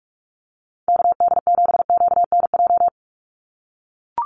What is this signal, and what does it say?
Morse code, heard as a keyed tone, about 44 wpm, 700 Hz. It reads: KD7QNJ
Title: USA Call Signs 45wpm